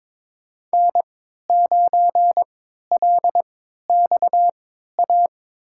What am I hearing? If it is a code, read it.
Morse code, heard as a keyed tone, about 22 wpm, spaced as for 13 wpm, 700 Hz. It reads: N9LXA
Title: USA Call Signs - 5 Characters - 3x Character Spacing - 22wpm